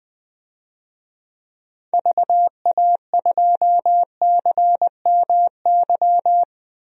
Morse code, heard as a keyed tone, about 21 wpm, 700 Hz. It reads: VA2CMY
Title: DXCC-100 Call Signs - 6 Characters - 20wpm